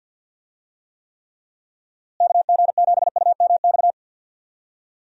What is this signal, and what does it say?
Morse code, heard as a keyed tone, about 51 wpm, 700 Hz. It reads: KG7UMX
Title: USA Call Signs - 6 Characters - Repeated 3x - 50wpm